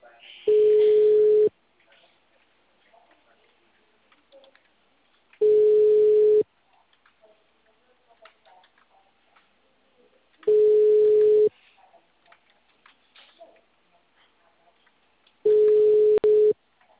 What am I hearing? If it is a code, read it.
Ringback tone (Europe, 425 Hz), with speech.